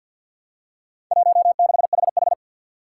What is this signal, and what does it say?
Morse code, heard as a keyed tone, about 50 wpm, 700 Hz. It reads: 16HH
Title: Four Challenging Letter-Number Combinations 50wpm